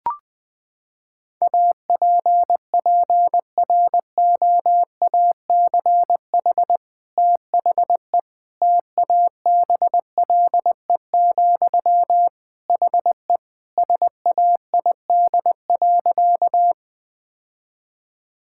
Morse code, heard as a keyed tone, about 20 wpm, 700 Hz. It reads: APPROACH THE TABLE, HE SAID.
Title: Sentences from Top 1000 Words - Review Part 6 - 20wpm